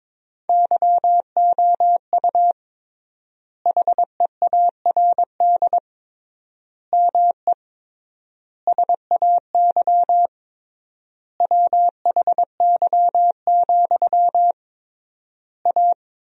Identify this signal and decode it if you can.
Morse code, heard as a keyed tone, about 21 wpm, 700 Hz. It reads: YOU HEARD ME SAY WHY, A
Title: 3x Word-Spacing - Sentences from Top 500 Words - 22wpm